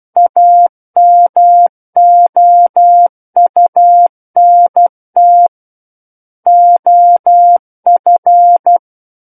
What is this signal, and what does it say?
Morse code, heard as a keyed tone, about 12 wpm, 700 Hz. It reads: AMOUNT OF